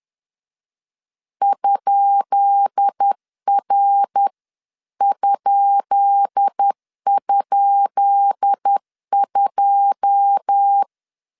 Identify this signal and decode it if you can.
Morse code, heard as a keyed tone, about 10 wpm, 800 Hz. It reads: ?R??2
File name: LEZIONI / 10 SIGNS / 10B INT 10wpm 2844